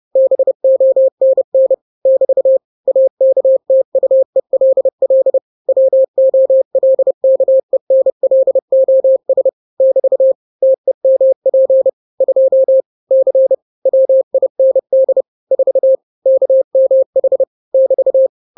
Morse code, heard as a keyed tone, about 30 wpm, 530 Hz. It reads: BONN <BT> AKTUELL WOLKENLOS <BT> TEMP 2 C WIND 4 KMH <BT>
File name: daily / 29wpm